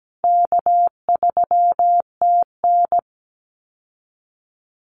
Morse code, heard as a keyed tone, about 17 wpm, 700 Hz. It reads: K3TN
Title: Easy Call Signs Repeated 2x - 17wpm